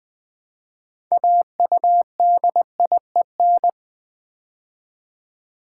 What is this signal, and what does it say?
Morse code, heard as a keyed tone, about 20 wpm, 700 Hz. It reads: AUDIEN
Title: Build A Word - 2 Letters At A Time - 8-Letter Words - 20wpm